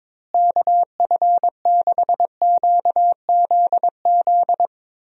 Morse code, heard as a keyed tone, about 22 wpm, 700 Hz. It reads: KF6QZZ
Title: USA Call Signs - 6 Characters - Repeated 2x - 22wpm